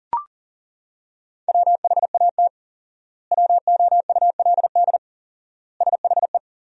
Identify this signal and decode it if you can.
Morse code, heard as a keyed tone, about 40 wpm, 700 Hz. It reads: WHAT WOULD SHE
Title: 4x Word-Spacing - Sentences from Top 100 Words - 40wpm